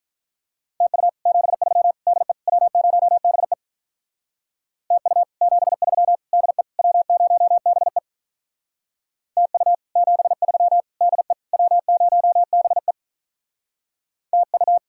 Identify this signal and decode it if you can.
Morse code, heard as a keyed tone, about 43 wpm, 700 Hz. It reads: TU 73 DE W0BE TU 73 DE W0BE TU 73 DE W0BE TU
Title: Speed-Racing Common QSO Phrases - 35wpm